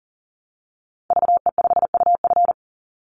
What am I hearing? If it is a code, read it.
Morse code, heard as a keyed tone, about 40 wpm, 700 Hz. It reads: VE5UF
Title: Call Signs Repeated 2x - 40wpm